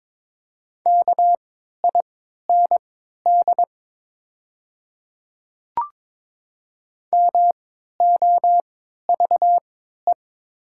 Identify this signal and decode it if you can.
Morse code, heard as a keyed tone, about 22 wpm, spaced as for 12 wpm, 700 Hz. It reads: KIND MOVE
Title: 3x Character Spacing - Top 200 Words - 22wpm